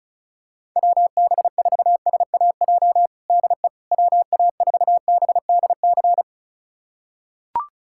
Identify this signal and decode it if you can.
Morse code, heard as a keyed tone, about 34 wpm, 700 Hz. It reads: WB4SAJ DE WA4BDC
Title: USA Callsign DE Callsign - 6 Characters - 35wpm